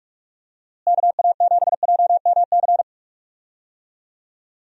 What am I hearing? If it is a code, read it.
Morse code, heard as a keyed tone, about 45 wpm, 700 Hz. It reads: KA7JMC
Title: USA Call Signs - 6 Characters - Repeated 2x - 45wpm